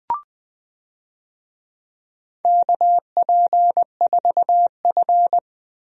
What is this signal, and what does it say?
Morse code, heard as a keyed tone, about 20 wpm, 700 Hz. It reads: KP4F
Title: DXCC-100 Call Signs - 4 Characters - 20wpm